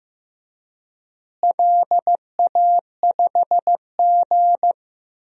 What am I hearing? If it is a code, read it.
Morse code, heard as a keyed tone, about 15 wpm, 700 Hz. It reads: LA5G
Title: DXCC-100 Call Signs - 4 Characters - Repeated 3x - 15wpm